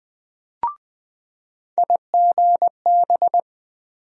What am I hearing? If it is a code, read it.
Morse code, heard as a keyed tone, about 20 wpm, 700 Hz. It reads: IGB
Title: USA Call Sign Postfixes - 3 Letters 20wpm